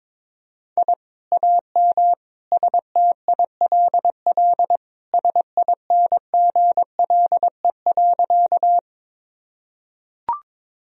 Morse code, heard as a keyed tone, about 22 wpm, 700 Hz. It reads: I AM STILL SINGLE.
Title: Sentences from Top 1000 Words - Review Part 4 - 22wpm